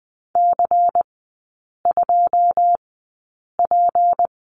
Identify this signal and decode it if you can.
Morse code, heard as a keyed tone, about 20 wpm, spaced as for 8 wpm, 700 Hz. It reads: C2P
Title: 218 Lesson 35e - Number 0 - Three Characters 20wpm